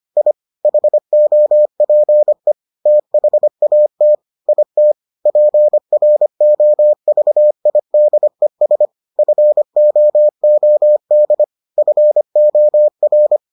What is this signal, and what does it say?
Morse code, heard as a keyed tone, about 25 wpm, 600 Hz. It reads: I HOPE THAT IT PROVIDES FOOD FOR